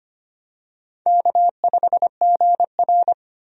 Morse code, heard as a keyed tone, about 25 wpm, 700 Hz. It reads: K5GR
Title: USA Call Signs - 4 Characters - 25wpm